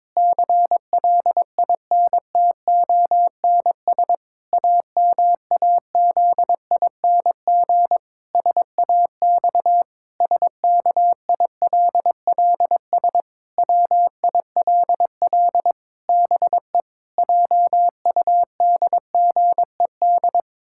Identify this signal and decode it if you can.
Morse code, heard as a keyed tone, about 22 wpm, 700 Hz. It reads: CLINTONS AMAZING SAX SKILLS WILL BE JUDGED
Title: Pangram Challenge v2 - 22wpm